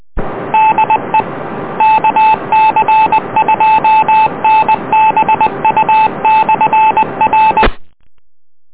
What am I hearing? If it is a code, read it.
Morse code, heard as a keyed tone, about 20 wpm, 870 Hz. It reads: DE KC2NBU/R